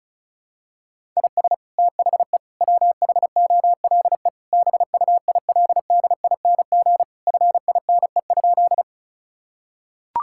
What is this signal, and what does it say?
Morse code, heard as a keyed tone, about 36 wpm, 700 Hz. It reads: IS THE WHOLE BUILDING FINE?
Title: Sentences from Top 1000 Words - Review Part 5 - 35wpm